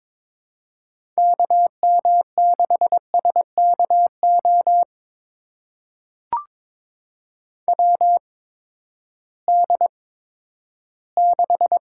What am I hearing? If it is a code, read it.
Morse code, heard as a keyed tone, about 22 wpm, spaced as for 12 wpm, 700 Hz. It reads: KM6SKO WD6
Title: USA Call Signs - 6 Characters - 8x Character Spacing - 22wpm